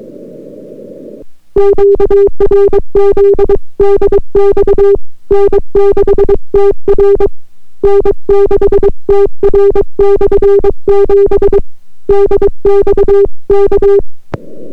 Morse code, heard as a keyed tone, about 23 wpm, 400 Hz. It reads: QRZ DX N6TR N6TR/7 DX K